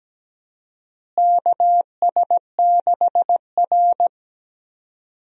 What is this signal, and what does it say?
Morse code, heard as a keyed tone, about 17 wpm, 700 Hz. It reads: KS6R
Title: USA Call Signs - 4 Characters - Repeated 2x - 17wpm